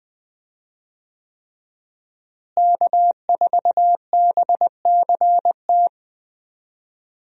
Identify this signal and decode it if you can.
Morse code, heard as a keyed tone, about 20 wpm, 700 Hz. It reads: K4BCT
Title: USA Call Signs - 5 Characters - Repeated 3x - 20wpm